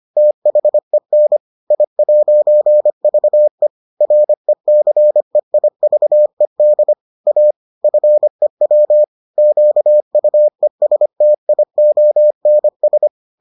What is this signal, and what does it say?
Morse code, heard as a keyed tone, about 25 wpm, 600 Hz. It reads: THEN I'VE RECEIVED A FEW QUESTIONS